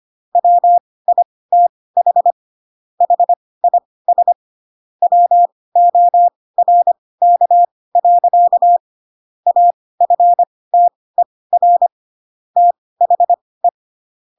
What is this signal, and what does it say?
Morse code, heard as a keyed tone, about 25 wpm, spaced as for 18 wpm, 700 Hz. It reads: WITH HIS WORK. AFTER THE
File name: Pinocchio ch3 12 25 ik1hge